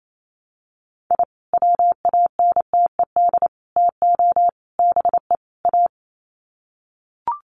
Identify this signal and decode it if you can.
Morse code, heard as a keyed tone, about 28 wpm, 700 Hz. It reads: I WANTED TO BE A
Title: Sets of 5 Words Encore2 28wpm